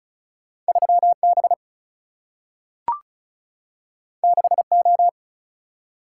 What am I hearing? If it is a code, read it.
Morse code, heard as a keyed tone, about 36 wpm, 700 Hz. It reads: 3B 6O
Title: Single-Letter Single-Number 35wpm